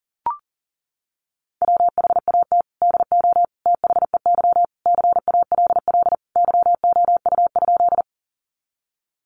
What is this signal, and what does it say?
Morse code, heard as a keyed tone, about 40 wpm, 700 Hz. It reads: WHAT DO THEY CALL YOU?
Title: Sentences from Top 100 Words 40wpm